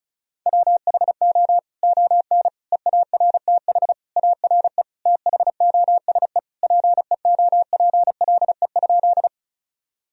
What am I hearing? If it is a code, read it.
Morse code, heard as a keyed tone, about 36 wpm, 700 Hz. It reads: WHO ON EARTH ARE THOSE PEOPLE?
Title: Sentences from Top 1000 Words - Review Part 1 - 35wpm